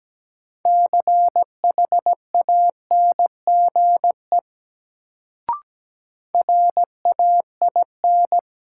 Morse code, heard as a keyed tone, about 17 wpm, 700 Hz. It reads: CHANGE RAIN
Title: Sentences from Top 600 Words - New Word Drill - 17wpm